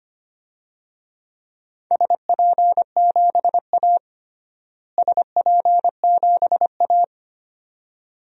Morse code, heard as a keyed tone, about 25 wpm, 700 Hz. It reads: SP7A SP7A
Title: DXCC-100 Call Signs - 4 Characters - Repeated 2x - 25wpm